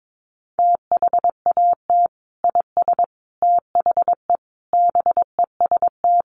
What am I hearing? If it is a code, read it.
Morse code, heard as a keyed tone, about 22 wpm, 700 Hz. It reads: THAT IS THE BEST
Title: Sentences from Top 1000 Words 22wpm